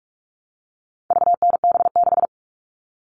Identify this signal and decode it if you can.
Morse code, heard as a keyed tone, about 44 wpm, 700 Hz. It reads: VNB6
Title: All Cut Numbers - 4-Digit 45wpm